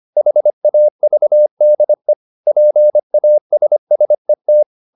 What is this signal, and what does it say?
Morse code, heard as a keyed tone, about 25 wpm, 600 Hz. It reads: HAVDE PASSET
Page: Ferie 25 wpm